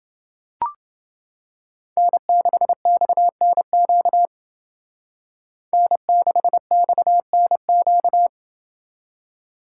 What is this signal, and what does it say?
Morse code, heard as a keyed tone, about 28 wpm, 700 Hz. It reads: N6XNQ N6XNQ